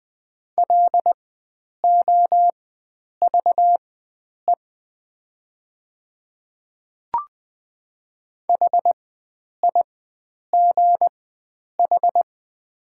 Morse code, heard as a keyed tone, about 20 wpm, spaced as for 9 wpm, 700 Hz. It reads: LOVE HIGH
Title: 4x Character Spacing - Top 500 Words - 20wpm